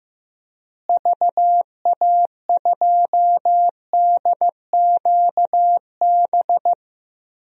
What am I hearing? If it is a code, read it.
Morse code, heard as a keyed tone, about 15 wpm, 700 Hz. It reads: VA2DQB
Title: DXCC-100 Call Signs - 6 Characters - Repeated 3x - 15wpm